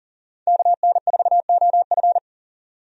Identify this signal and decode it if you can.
Morse code, heard as a keyed tone, about 40 wpm, 700 Hz. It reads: KN4OF
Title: USA Call Signs - 5 Characters - 40wpm